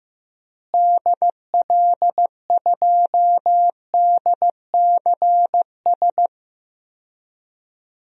Morse code, heard as a keyed tone, about 15 wpm, 700 Hz. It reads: DL2DCS
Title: DXCC-100 Call Signs - 6 Characters - Repeated 3x - 15wpm